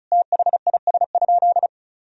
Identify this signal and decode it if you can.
Morse code, heard as a keyed tone, about 34 wpm, 700 Hz. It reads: THIS?
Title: Sentences from Top 200 Words and Abbreviations 35wpm